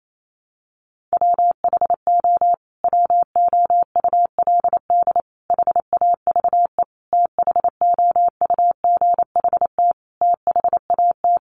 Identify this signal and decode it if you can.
Morse code, heard as a keyed tone, about 28 wpm, 700 Hz. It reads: WHO WOULD HAVE THOUGHT THAT
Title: Sentences from Top 1200 Words 28wpm